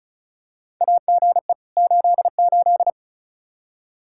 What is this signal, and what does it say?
Morse code, heard as a keyed tone, about 34 wpm, 700 Hz. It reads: AGE 88
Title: Practice QSO - Exchange 3 - More Info - 35wpm